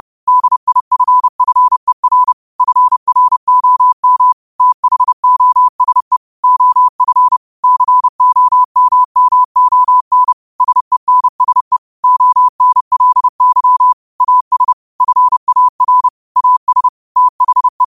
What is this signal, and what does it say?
Morse code, heard as a keyed tone, about 30 wpm, 1000 Hz. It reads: DIFFER FROM THOSE OF COMMON SENSE ONLY AS FAR AS THE
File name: qotdfiles / 30wpm